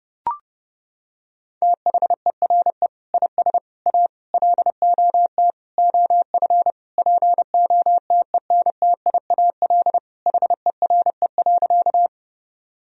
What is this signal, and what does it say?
Morse code, heard as a keyed tone, about 30 wpm, 700 Hz. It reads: THERE IS A LOT OF POTENTIAL HERE.
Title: Sentences from Top 1000 Words - Review Part 4 - 30wpm